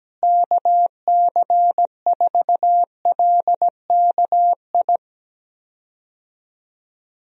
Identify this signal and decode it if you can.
Morse code, heard as a keyed tone, about 17 wpm, 700 Hz. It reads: KC4LKI
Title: USA Call Signs - 6 Characters - Repeated 3x - 17wpm